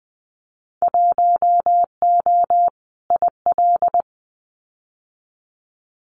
Morse code, heard as a keyed tone, about 20 wpm, 700 Hz. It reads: 1O IL
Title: Winter Field Day Exchanges 20wpm